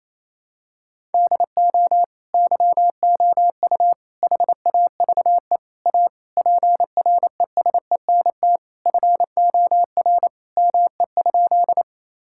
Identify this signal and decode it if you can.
Morse code, heard as a keyed tone, about 28 wpm, 700 Hz. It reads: DO YOU HAVE A PRESENT FOR ME?
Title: Sentences from Top 1000 Words - Review Part 5 - 28wpm